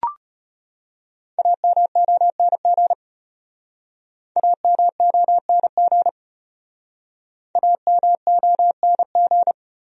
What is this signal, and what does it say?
Morse code, heard as a keyed tone, about 33 wpm, 700 Hz. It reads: AMONG AMONG AMONG